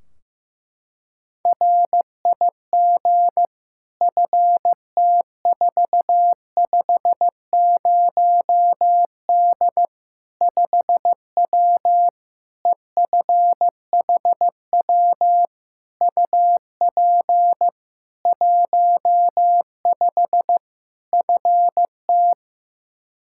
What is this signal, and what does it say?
Morse code, heard as a keyed tone, about 15 wpm, 700 Hz. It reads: RIG FT450D 5W EFHW UP 15 FT
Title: Practice QSO - Exchange 2 - Info - 15wpm